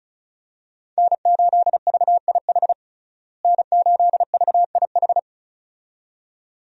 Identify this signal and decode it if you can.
Morse code, heard as a keyed tone, about 36 wpm, 700 Hz. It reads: N8VIH N8VIH
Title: USA Call Signs - 5 Characters - Repeated 2x - 35wpm